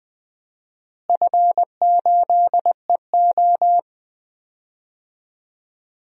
Morse code, heard as a keyed tone, about 20 wpm, 700 Hz. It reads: F8EO
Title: DXCC-100 Call Signs - 4 Characters - Repeated 3x - 20wpm